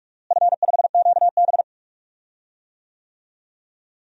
Morse code, heard as a keyed tone, about 41 wpm, 700 Hz. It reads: F5QB
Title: DXCC-100 Call Signs - 4 Characters - Repeated 3x - 45wpm